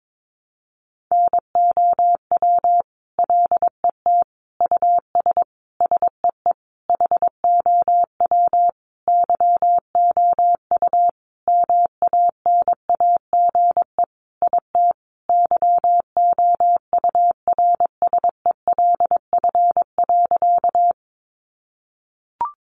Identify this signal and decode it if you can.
Morse code, heard as a keyed tone, about 22 wpm, 700 Hz. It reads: NOW LET US SEE HOW YOU MANAGE IT YOURSELF.
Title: Sentences from Top 1400 Words 22wpm